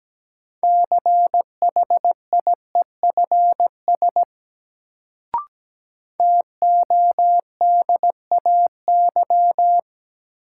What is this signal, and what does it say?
Morse code, heard as a keyed tone, about 17 wpm, 700 Hz. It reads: CHIEFS TODAY
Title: Sets of 5 Words Encore - New Word Drill - 17wpm